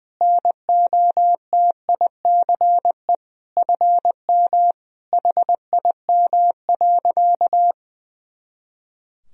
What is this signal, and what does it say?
Morse code, heard as a keyed tone, about 21 wpm, 700 Hz. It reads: NOTICE FM HIM.
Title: Sentences from Top 400 Words and Abbreviations 20wpm